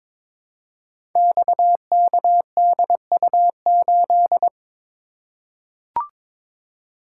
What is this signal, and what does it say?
Morse code, heard as a keyed tone, about 22 wpm, 700 Hz. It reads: XKDU8
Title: Code Groups - 5 Letter:Numbers - 22wpm